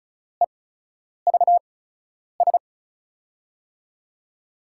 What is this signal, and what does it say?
Morse code, heard as a keyed tone, about 36 wpm, spaced as for 10 wpm, 700 Hz. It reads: EVS
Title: USA Call Signs - 8x Character Spacing - 35wpm